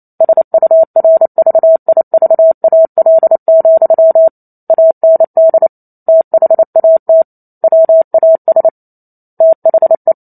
Morse code, heard as a keyed tone, about 28 wpm, 650 Hz. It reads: SURVIVAL, AND THAT WAS THE
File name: Chapter 2 - 28 WPM